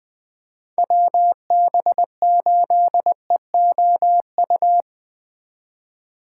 Morse code, heard as a keyed tone, about 23 wpm, 700 Hz. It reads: WB8EOU
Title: Speed Racing - USA 2x3 Call Signs 20wpm